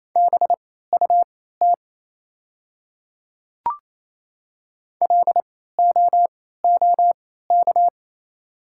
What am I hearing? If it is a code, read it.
Morse code, heard as a keyed tone, about 27 wpm, spaced as for 16 wpm, 700 Hz. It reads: BUT LOOK
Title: 3x Character Spacing - Top 200 Words - 28wpm